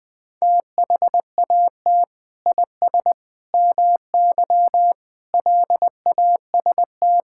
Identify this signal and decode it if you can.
Morse code, heard as a keyed tone, about 20 wpm, 700 Hz. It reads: THAT IS MY LAST
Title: Sentences from Top 1600 Words 20wpm